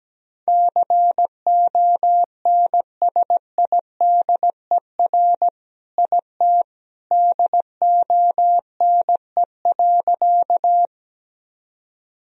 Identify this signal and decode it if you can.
Morse code, heard as a keyed tone, about 17 wpm, 700 Hz. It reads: CONSIDER IT DONE.
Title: Sentences from Top 1000 Words - Review Part 4 - 17wpm